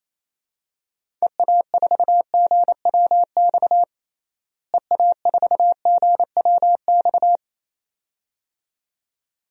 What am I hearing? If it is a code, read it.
Morse code, heard as a keyed tone, about 28 wpm, 700 Hz. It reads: EA4GWX EA4GWX
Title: DXCC-100 Call Signs - 6 Characters - Repeated 2x - 28wpm